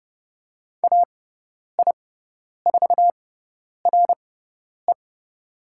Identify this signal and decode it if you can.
Morse code, heard as a keyed tone, about 30 wpm, spaced as for 10 wpm, 700 Hz. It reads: AI4RE
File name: Lesson 16g - Letter c - Call Signs 30wpm